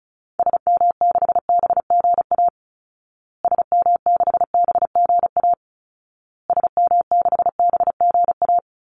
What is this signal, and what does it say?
Morse code, heard as a keyed tone, about 34 wpm, 700 Hz. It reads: SM6BGA SM6BGA SM6BGA
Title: Call Signs Repeated 2x - 35wpm